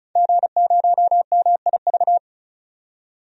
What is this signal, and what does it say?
Morse code, heard as a keyed tone, about 36 wpm, 700 Hz. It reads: G0MIV
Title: DXCC-100 Call Signs - 5 Characters - 35wpm